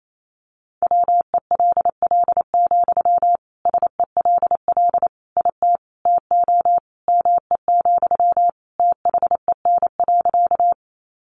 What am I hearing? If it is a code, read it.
Morse code, heard as a keyed tone, about 28 wpm, 700 Hz. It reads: WELL, SELL IT TO ME, THEN.
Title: Sentences from Top 1100 Words 28wpm